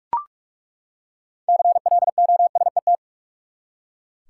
Morse code, heard as a keyed tone, about 42 wpm, 700 Hz. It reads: CLOSET
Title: Six Letter Words - 45wpm